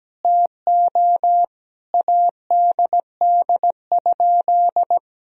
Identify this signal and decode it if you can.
Morse code, heard as a keyed tone, about 17 wpm, 700 Hz. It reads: TO ADD?
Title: Sentences from Top 1000 Words - Review Part 1 - 17wpm